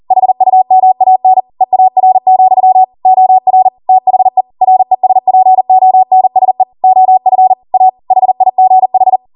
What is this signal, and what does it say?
Morse code, heard as a keyed tone, about 40 wpm, 750 Hz. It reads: HUMAN EAR, OR THE RESPONSE OF A HIGH